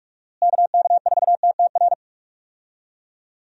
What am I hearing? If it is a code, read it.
Morse code, heard as a keyed tone, about 46 wpm, 700 Hz. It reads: KK4TTR
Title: USA Call Signs - 6 Characters - Repeated 2x - 45wpm